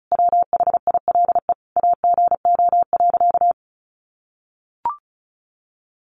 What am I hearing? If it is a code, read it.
Morse code, heard as a keyed tone, about 34 wpm, 700 Hz. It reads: WHILE AGO.
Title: Sentences from Top 600 Words 35wpm